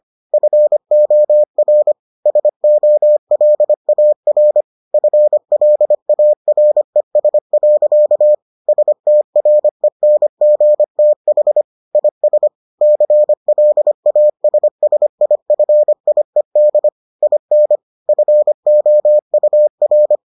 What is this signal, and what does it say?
Morse code, heard as a keyed tone, about 25 wpm, 600 Hz. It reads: FOR SOLAR FLARES. STRENGTH IS CLASSIFIED IN FOUR